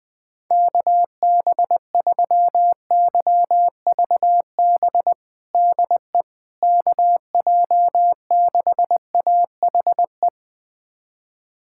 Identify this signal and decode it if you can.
Morse code, heard as a keyed tone, about 20 wpm, 700 Hz. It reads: KB3YVB DE KJ6AHE
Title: USA Callsign DE Callsign - 6 Characters - 20wpm